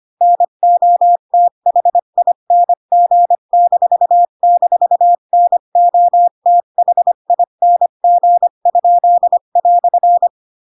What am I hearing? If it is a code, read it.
Morse code, heard as a keyed tone, about 25 wpm, 700 Hz. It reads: NOTHING--NOTHING?"
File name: Pinocchio / Pinocchio ch7 20 25 ik1hge